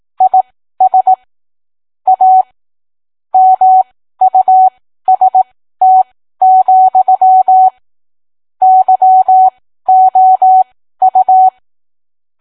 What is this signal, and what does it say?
Morse code, heard as a keyed tone, about 18 wpm, spaced as for 13 wpm, 750 Hz. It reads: IS A MUST, YOU